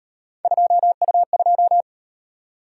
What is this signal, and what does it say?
Morse code, heard as a keyed tone, about 38 wpm, 700 Hz. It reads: 2U2